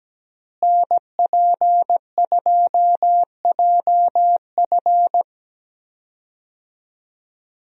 Morse code, heard as a keyed tone, about 17 wpm, 700 Hz. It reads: NP2JF
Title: USA Call Signs - 5 Characters - Repeated 2x - 17wpm